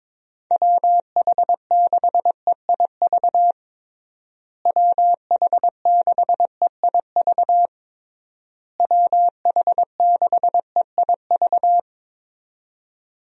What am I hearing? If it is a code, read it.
Morse code, heard as a keyed tone, about 22 wpm, 700 Hz. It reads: WH6EIV WH6EIV WH6EIV
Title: USA Call Signs - 6 Characters - Repeated 3x - 22wpm